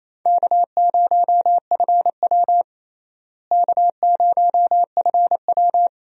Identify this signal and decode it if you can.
Morse code, heard as a keyed tone, about 28 wpm, 700 Hz. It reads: K0FW K0FW
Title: USA Call Signs - 4 Characters - Repeated 2x - 28wpm